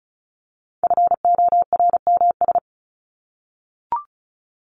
Morse code, heard as a keyed tone, about 34 wpm, 700 Hz. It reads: FORMS
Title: ES:S Suffix - Top 100 Words with Base Words 35wpm